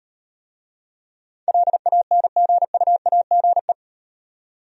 Morse code, heard as a keyed tone, about 38 wpm, 700 Hz. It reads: LANGUAGE